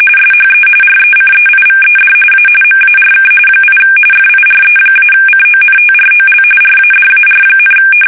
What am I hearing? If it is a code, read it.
RTTY, about 45.45 baud. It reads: SHE RECEIVED A MARK 8 FIRE-CONTROL RADAR,
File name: fsk850 / USS-Massachusetts